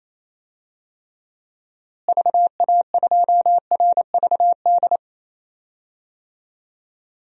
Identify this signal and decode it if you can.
Morse code, heard as a keyed tone, about 28 wpm, 700 Hz. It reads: VA2RVD
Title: DXCC-100 Call Signs - 6 Characters - Repeated 2x - 28wpm